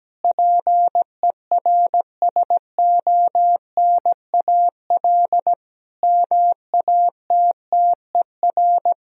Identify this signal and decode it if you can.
Morse code, heard as a keyed tone, about 17 wpm, 700 Hz. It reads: PERSONAL MATTER
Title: Sentences from Top 1000 Words - Review Part 4 - 17wpm